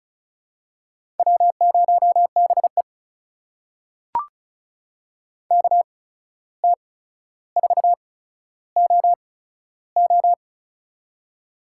Morse code, heard as a keyed tone, about 36 wpm, spaced as for 14 wpm, 700 Hz. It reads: W0BE KT4OO
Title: USA Call Signs - 8x Character Spacing - 35wpm